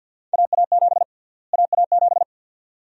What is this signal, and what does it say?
Morse code, heard as a keyed tone, about 52 wpm, 700 Hz. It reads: AA7 AA7
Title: USA Call Sign Prefixes - Repeated 2x - 50wpm